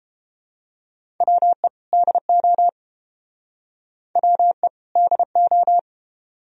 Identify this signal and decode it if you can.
Morse code, heard as a keyed tone, about 30 wpm, 700 Hz. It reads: WE DO WE DO